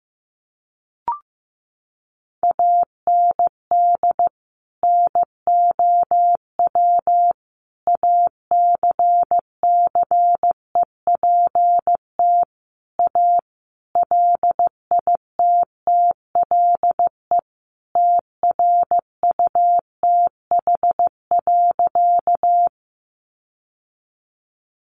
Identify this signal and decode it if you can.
Morse code, heard as a keyed tone, about 15 wpm, 700 Hz. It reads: AND NOW ACCEPT A LITTLE TRUTH.
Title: Sentences from Top 1300 Words 15wpm